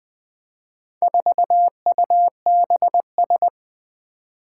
Morse code, heard as a keyed tone, about 20 wpm, 700 Hz. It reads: 4UBS
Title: Four Challenging Letter-Number Combinations 20wpm